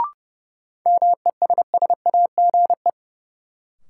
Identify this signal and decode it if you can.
Morse code, heard as a keyed tone, about 32 wpm, 700 Hz. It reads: MESSAGE
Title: Sentences from Top 1100 Words - New Word Drill - 30wpm